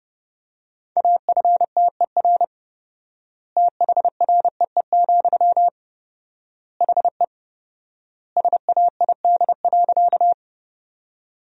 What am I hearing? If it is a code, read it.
Morse code, heard as a keyed tone, about 30 wpm, 700 Hz. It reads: AFTER THREE, HE SAID.
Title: 4x Word-Spacing - Sentences from Top 200 Words - 30wpm